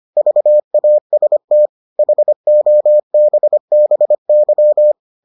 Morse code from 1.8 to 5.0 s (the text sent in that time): HOBBY